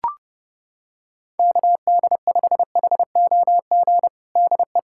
Morse code, heard as a keyed tone, about 28 wpm, 700 Hz. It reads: KD5HOG DE
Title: USA Callsign DE Callsign - 6 Characters - 30wpm